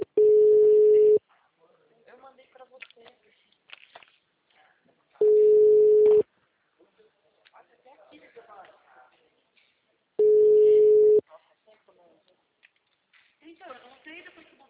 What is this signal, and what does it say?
Ringback tone (Europe, 425 Hz), with speech.
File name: out- -2017- - - .13317